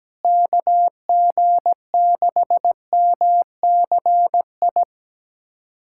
Morse code, heard as a keyed tone, about 17 wpm, 700 Hz. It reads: KG6MCI
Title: USA Callsign DE Callsign 17wpm